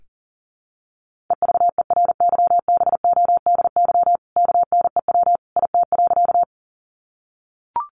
Morse code, heard as a keyed tone, about 40 wpm, 700 Hz. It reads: EVERYBODY KNEW IT.
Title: Sentences from Top 1400 Words 40wpm